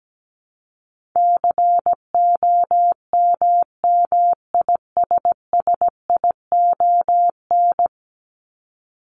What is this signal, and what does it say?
Morse code, heard as a keyed tone, about 17 wpm, 700 Hz. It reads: COMMISSION
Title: Top 1251-1500 Words 17wpm